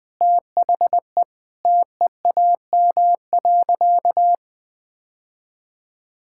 Morse code, heard as a keyed tone, about 20 wpm, 700 Hz. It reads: THE TEAM.
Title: Sentences with 5-Word Phrases - 20wpm